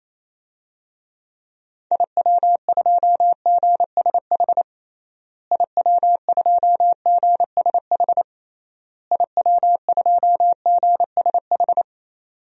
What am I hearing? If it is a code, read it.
Morse code, heard as a keyed tone, about 28 wpm, 700 Hz. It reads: IW2GSH IW2GSH IW2GSH
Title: DXCC-100 Call Signs - 6 Characters - Repeated 3x - 28wpm